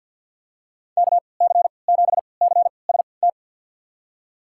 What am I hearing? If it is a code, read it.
Morse code, heard as a keyed tone, about 50 wpm, spaced as for 28 wpm, 700 Hz. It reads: KC7CST
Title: USA Call Signs - 3x Character Spacing - 50wpm